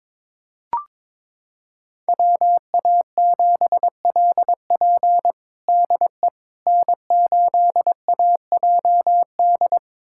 Morse code, heard as a keyed tone, about 22 wpm, 700 Hz. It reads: WA7LP DE N8AJD
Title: USA Callsign DE Callsign - 5 Characters - 22wpm